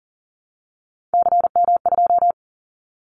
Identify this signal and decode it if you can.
Morse code, heard as a keyed tone, about 40 wpm, 700 Hz. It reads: CM2
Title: Callsign Prefixes for Top 100 DXCC - 40wpm